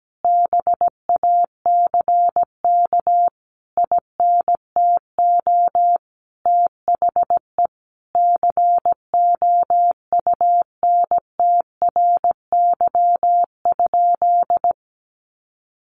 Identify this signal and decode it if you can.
Morse code, heard as a keyed tone, about 17 wpm, 700 Hz. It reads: BACK INTO THE COUNTRY?
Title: Sentences from Top 1400 Words 17wpm